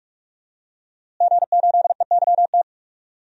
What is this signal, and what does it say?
Morse code, heard as a keyed tone, about 46 wpm, 700 Hz. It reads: G8EYT
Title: DXCC-100 Call Signs - 5 Characters - Repeated 3x - 45wpm